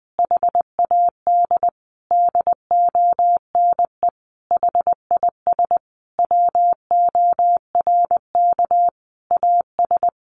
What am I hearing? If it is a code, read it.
Morse code, heard as a keyed tone, about 20 wpm, 700 Hz. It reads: HAD DONE HIS WORK AS
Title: Sentences from Top 500 Words 20wpm